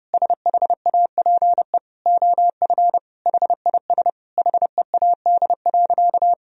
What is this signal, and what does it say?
Morse code, heard as a keyed tone, about 30 wpm, 700 Hz. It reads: SHAPE OF HIS HEAD.
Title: Sentences from Top 1000 Words - Review Part 3 - 30wpm